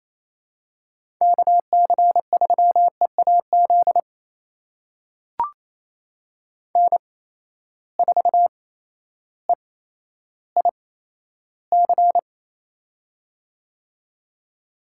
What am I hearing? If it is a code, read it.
Morse code, heard as a keyed tone, about 28 wpm, spaced as for 13 wpm, 700 Hz. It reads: KC3EAZ N4EIC
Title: USA Call Signs - 8x Character Spacing - 28wpm